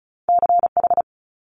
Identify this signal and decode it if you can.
Morse code, heard as a keyed tone, about 36 wpm, 700 Hz. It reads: CH
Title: Multi-letter Phonemes Quick Fire 35wpm